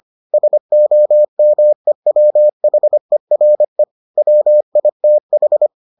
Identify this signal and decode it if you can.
Morse code, heard as a keyed tone, about 25 wpm, 600 Hz. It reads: SOMEWHERE WITH